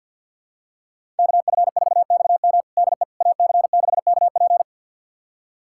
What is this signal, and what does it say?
Morse code, heard as a keyed tone, about 50 wpm, 700 Hz. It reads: KF4XM DE AC6KP
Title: USA Callsign DE Callsign - 5 Characters - 50wpm